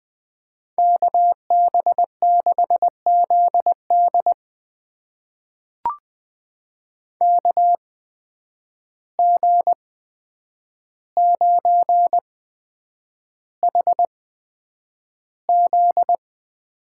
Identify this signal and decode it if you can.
Morse code, heard as a keyed tone, about 21 wpm, spaced as for 9 wpm, 700 Hz. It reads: KB6ZD KG9HZ
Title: USA Call Signs - 5 Characters - 8x Character Spacing - 20wpm